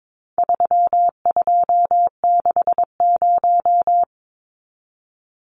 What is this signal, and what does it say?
Morse code, heard as a keyed tone, about 22 wpm, 700 Hz. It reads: 3260
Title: Numbers - 4-Digit 22wpm